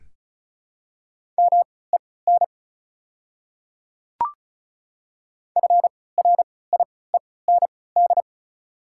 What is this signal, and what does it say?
Morse code, heard as a keyed tone, about 34 wpm, spaced as for 20 wpm, 690 Hz. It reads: MEN FRIEND
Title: 3x Character Spacing - Top 500 Words - 35wpm